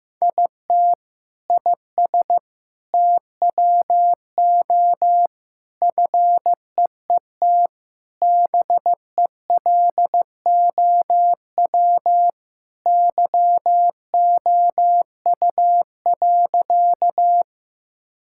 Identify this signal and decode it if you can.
Morse code, heard as a keyed tone, about 15 wpm, 700 Hz. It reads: IT IS TWO FEET BELOW YOU.
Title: Sentences from Top 1000 Words - Review Part 5 - 15wpm